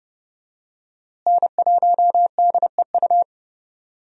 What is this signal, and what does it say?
Morse code, heard as a keyed tone, about 30 wpm, 700 Hz. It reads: N1DEU
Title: USA Call Signs - 5 Characters - Repeated 2x - 30wpm